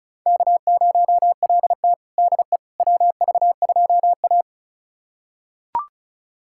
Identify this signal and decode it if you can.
Morse code, heard as a keyed tone, about 36 wpm, 700 Hz. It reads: K0LT DE WV2A
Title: USA Callsign DE Callsign - 4 Characters - 35wpm